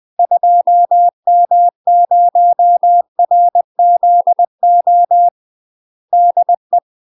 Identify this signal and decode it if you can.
Morse code, heard as a keyed tone, about 20 wpm, 700 Hz. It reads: 2M0RZO DE
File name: UK MASTER QSOs TRAINING / 20WPM / UK MASTER QSOs 0040